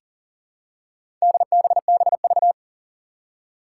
Morse code, heard as a keyed tone, about 40 wpm, 700 Hz. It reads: DBBV
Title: Four Challenging Letter-Number Combinations 40wpm